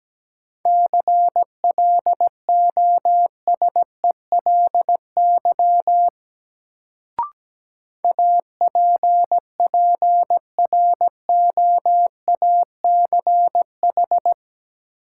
Morse code, heard as a keyed tone, about 17 wpm, 700 Hz. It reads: CLOSELY APPROACH
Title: Sentences from Top 1400 Words - New Word Drill Alt - 17wpm